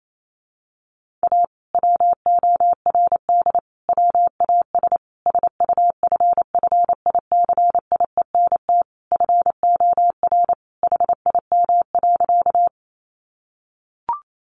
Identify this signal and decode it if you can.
Morse code, heard as a keyed tone, about 28 wpm, 700 Hz. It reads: A WORD WAS SUFFICIENT FOR HIM.
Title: Sentences from Top 1400 Words 28wpm